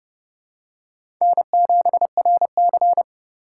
Morse code, heard as a keyed tone, about 29 wpm, 700 Hz. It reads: N7RC
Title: USA Call Signs - 4 Characters - 3x Character Spacing - 30wpm